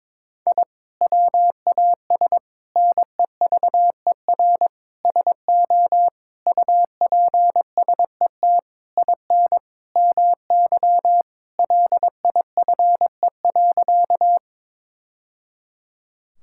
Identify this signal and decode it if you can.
Morse code, heard as a keyed tone, about 22 wpm, 700 Hz. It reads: I WAS NEVER SO UPSET IN MY LIFE.
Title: Sentences from Top 1000 Words - Review Part 5 - 22wpm